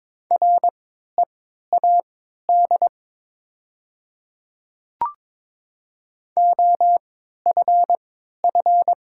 Morse code, heard as a keyed tone, about 22 wpm, spaced as for 12 wpm, 700 Hz. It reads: READ OFF
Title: 3x Character Spacing - Top 300 Words - 22wpm